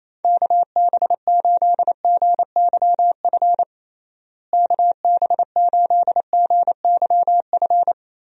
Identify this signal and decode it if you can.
Morse code, heard as a keyed tone, about 28 wpm, 700 Hz. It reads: KB8GYF KB8GYF
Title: USA Call Signs - 6 Characters - Repeated 3x - 28wpm